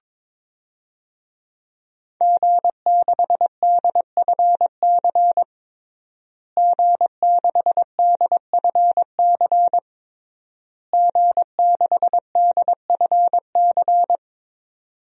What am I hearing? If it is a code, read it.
Morse code, heard as a keyed tone, about 21 wpm, 700 Hz. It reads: G6DFC G6DFC G6DFC
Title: DXCC-100 Call Signs - 5 Characters - Repeated 3x - 22wpm